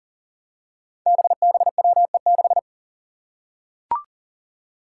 Morse code, heard as a keyed tone, about 40 wpm, 700 Hz. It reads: BBWE6
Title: Five Challenging Letter-Number Combinations 40wpm